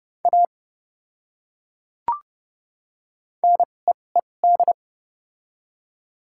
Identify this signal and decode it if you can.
Morse code, heard as a keyed tone, about 32 wpm, spaced as for 22 wpm, 700 Hz. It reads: A NEED
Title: 2x Character Spacing - Top 300 Words - 30wpm